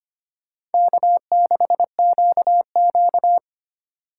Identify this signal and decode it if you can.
Morse code, heard as a keyed tone, about 31 wpm, 700 Hz. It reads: K6QQ
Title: Speed Racing - USA 1x2 Call Signs 25wpm